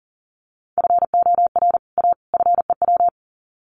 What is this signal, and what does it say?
Morse code, heard as a keyed tone, about 40 wpm, 700 Hz. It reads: FOR A FEW
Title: Sets of 3 Words 40wpm